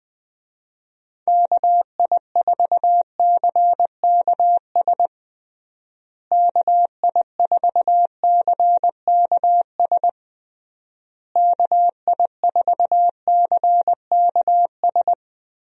Morse code, heard as a keyed tone, about 21 wpm, 700 Hz. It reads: KI4CKS KI4CKS KI4CKS
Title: USA Call Signs - 6 Characters - Repeated 3x - 20wpm